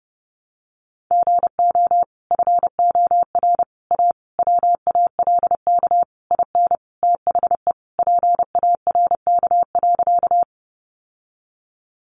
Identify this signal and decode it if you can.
Morse code, heard as a keyed tone, about 30 wpm, 700 Hz. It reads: GO FOR A WALK IN THE PARK.
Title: Sentences from Top 1300 Words 30wpm